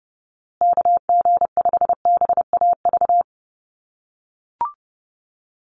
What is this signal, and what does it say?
Morse code, heard as a keyed tone, about 30 wpm, 700 Hz. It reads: KG5BAV
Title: USA Call Signs Repeated 2x 30wpm